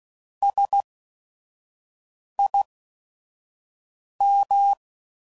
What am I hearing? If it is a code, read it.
Morse code, heard as a keyed tone, about 16 wpm, spaced as for 5 wpm, 800 Hz. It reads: SIM
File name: Prefixes / By First Letter / S vs Hs